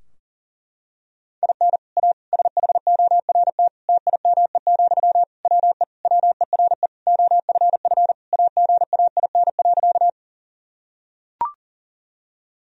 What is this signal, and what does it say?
Morse code, heard as a keyed tone, about 40 wpm, 700 Hz. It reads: IN A SHORT TIME, WE WERE OFF AGAIN.
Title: Sentences from Top 1000 Words - Review Part 2 - 40wpm